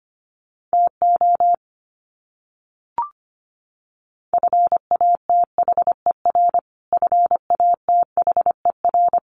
Morse code, heard as a keyed tone, about 25 wpm, 700 Hz. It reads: TO FATHER FATHER
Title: Top 300 Words Repeated 25wpm